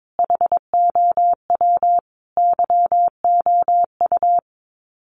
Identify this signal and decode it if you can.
Morse code, heard as a keyed tone, about 22 wpm, 700 Hz. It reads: HOW YOU
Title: Sets of 5 Words Encore 22wpm